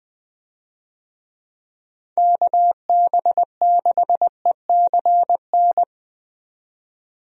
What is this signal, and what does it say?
Morse code, heard as a keyed tone, about 20 wpm, 700 Hz. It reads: KB6ECN
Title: USA Call Signs - 6 Characters - 20wpm